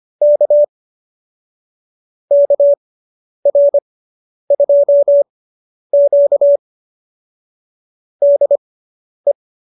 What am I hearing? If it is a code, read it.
Morse code, heard as a keyed tone, about 25 wpm, spaced as for 10 wpm, 590 Hz. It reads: K KR2Q DE